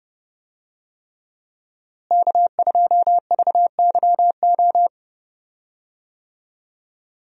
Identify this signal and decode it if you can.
Morse code, heard as a keyed tone, about 31 wpm, 700 Hz. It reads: K2VYO
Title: USA Call Signs - 5 Characters - Repeated 3x - 30wpm